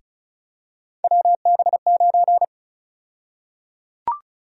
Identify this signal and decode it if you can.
Morse code, heard as a keyed tone, about 36 wpm, 700 Hz. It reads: WB9
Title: USA Call Sign Prefixes - Repeated 2x - 35wpm